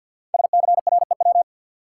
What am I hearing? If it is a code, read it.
Morse code, heard as a keyed tone, about 53 wpm, 700 Hz. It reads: SCREW
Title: Sentences from Top 1600 Words - New Word Drill Alt - 50wpm